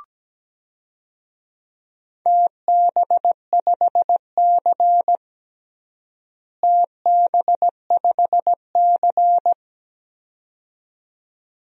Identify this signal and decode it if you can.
Morse code, heard as a keyed tone, about 17 wpm, 700 Hz. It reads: TB5C TB5C
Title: DXCC-100 Call Signs - 4 Characters - Repeated 2x - 17wpm